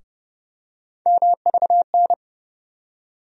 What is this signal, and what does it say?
Morse code, heard as a keyed tone, about 30 wpm, 700 Hz. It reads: MVN
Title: USA Call Sign Postfixes - 3 Letters 30wpm